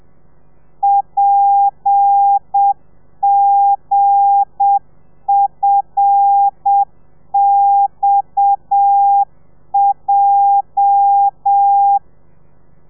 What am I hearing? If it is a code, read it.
Morse code, heard as a keyed tone, about 7 wpm, 800 Hz. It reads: PGFXJ
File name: telegrafia / i2tfj / Lez8 30